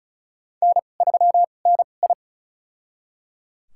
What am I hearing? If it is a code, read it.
Morse code, heard as a keyed tone, about 36 wpm, spaced as for 26 wpm, 700 Hz. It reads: N3NI
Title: USA Call Signs - 4 Characters - 2x Character Spacing - 35wpm